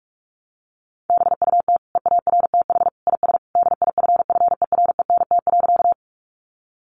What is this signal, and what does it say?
Morse code, heard as a keyed tone, about 44 wpm, 700 Hz. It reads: BUT EARTH IS DIFFERENT.
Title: Sentences from Top 600 Words 45wpm